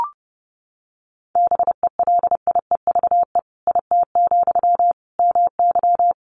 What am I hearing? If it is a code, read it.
Morse code, heard as a keyed tone, about 30 wpm, 700 Hz. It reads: BELIEVE IT, MY
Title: Sentences from Top 1000 Words 30wpm